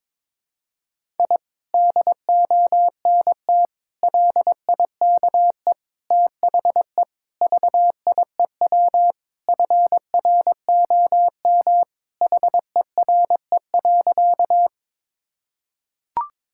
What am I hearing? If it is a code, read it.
Morse code, heard as a keyed tone, about 22 wpm, 700 Hz. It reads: I DONT LIKE THE VIEW FROM HERE.
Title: Sentences from Top 1000 Words - Review Part 6 - 22wpm